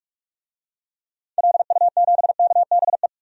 Morse code, heard as a keyed tone, about 46 wpm, 700 Hz. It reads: LU7KBE
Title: DXCC-100 Call Signs - 6 Characters - 45wpm